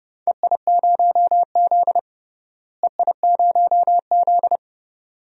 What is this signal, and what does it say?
Morse code, heard as a keyed tone, about 30 wpm, 700 Hz. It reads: EI0Z EI0Z
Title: DXCC-100 Call Signs - 4 Characters - Repeated 2x - 30wpm